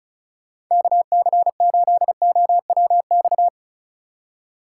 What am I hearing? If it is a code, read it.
Morse code, heard as a keyed tone, about 33 wpm, 700 Hz. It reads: KC8OWX
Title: Speed Racing - USA 2x3 Call Signs 35wpm